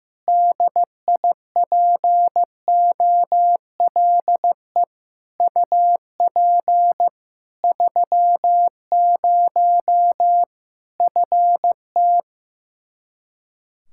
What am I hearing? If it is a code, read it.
Morse code, heard as a keyed tone, about 15 wpm, 700 Hz. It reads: DIPOLE UP 30 FT
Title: Practice QSO - Exchange 2 - Info - 15wpm